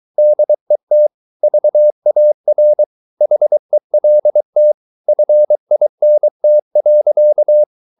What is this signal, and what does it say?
Morse code, heard as a keyed tone, about 23 wpm, 600 Hz. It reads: DET VAR HELT FINT.
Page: Ferie 23 wpm